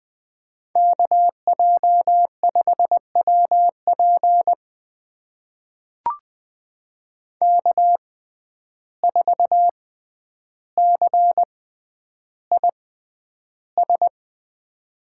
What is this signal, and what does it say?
Morse code, heard as a keyed tone, about 20 wpm, spaced as for 10 wpm, 700 Hz. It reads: KJ5WP K4CIS
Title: USA Call Signs - 6x Character Spacing - 20wpm